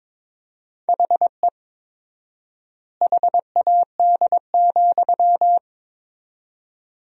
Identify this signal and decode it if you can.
Morse code, heard as a keyed tone, about 22 wpm, 700 Hz. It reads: HE HAD,
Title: 4x Word-Spacing - Sentences from Top 500 Words - 22wpm